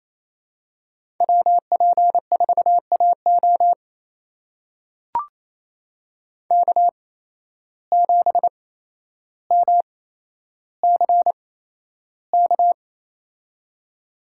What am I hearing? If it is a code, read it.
Morse code, heard as a keyed tone, about 28 wpm, spaced as for 12 wpm, 700 Hz. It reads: WP4AO K7MCK
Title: USA Call Signs - 5 Characters - 8x Character Spacing - 28wpm